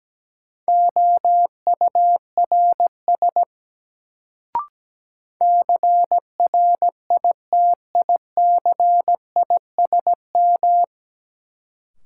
Morse code, heard as a keyed tone, about 17 wpm, 700 Hz. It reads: OURS CRITICISM
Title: Sentences from Top 1300 Words - New Word Drill Alt - 17wpm